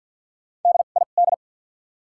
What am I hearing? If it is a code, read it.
Morse code, heard as a keyed tone, about 54 wpm, spaced as for 37 wpm, 700 Hz. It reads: DID
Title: 2x Character Spacing - Top 500 Words - 50wpm